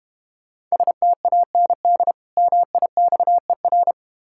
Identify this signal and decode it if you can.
Morse code, heard as a keyed tone, about 31 wpm, 700 Hz. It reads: STAND MIXER